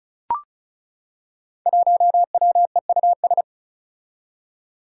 Morse code, heard as a keyed tone, about 34 wpm, 700 Hz. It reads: 1WEUS
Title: Five Challenging Letter-Number Combinations 35wpm